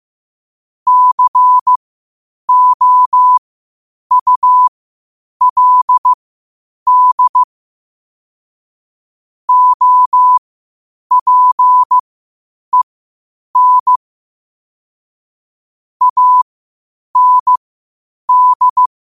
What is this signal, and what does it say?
Morse code, heard as a keyed tone, about 15 wpm, spaced as for 8 wpm, 1000 Hz. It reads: COULD OPEN AND
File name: qotdfiles / 8wpm